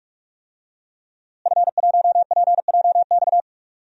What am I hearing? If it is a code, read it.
Morse code, heard as a keyed tone, about 46 wpm, 700 Hz. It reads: F1PJX
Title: DXCC-100 Call Signs - 5 Characters - Repeated 3x - 45wpm